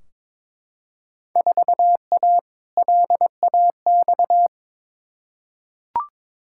Morse code, heard as a keyed tone, about 22 wpm, 700 Hz. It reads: 4A LAX
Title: Field Day Exchanges - 22wpm